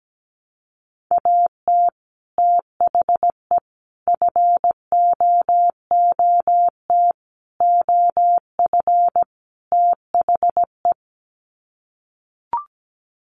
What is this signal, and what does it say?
Morse code, heard as a keyed tone, about 17 wpm, 700 Hz. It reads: AT THE FOOT OF THE
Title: Sets of 5 Words 17wpm